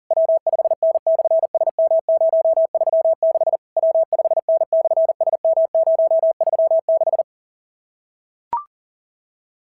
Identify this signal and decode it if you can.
Morse code, heard as a keyed tone, about 40 wpm, 660 Hz. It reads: W5N/SM036 W5N/SM036
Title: SOTA QSO Practice 40wpm